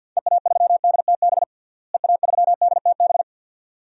Morse code, heard as a keyed tone, about 47 wpm, 700 Hz. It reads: EA3DTB EA3DTB
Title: DXCC-100 Call Signs - 6 Characters - Repeated 2x - 50wpm